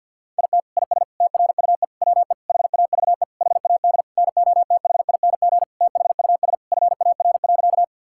Morse code, heard as a keyed tone, about 46 wpm, 690 Hz. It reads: IT IS TRUE WE HAVE HAD NOTHING THUS FAR.
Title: Sentences from Top 1000 Words - Review Part 5 - 50wpm